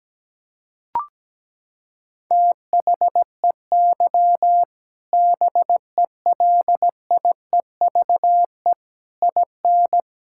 Morse code, heard as a keyed tone, about 17 wpm, 700 Hz. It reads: THEY BELIEVE IN
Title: Sentences from Top 1000 Words - Review Part 5 - 17wpm